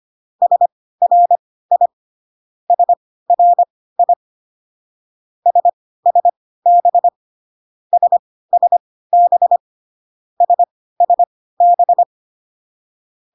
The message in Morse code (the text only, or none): SRI SRI SSB SSB SSB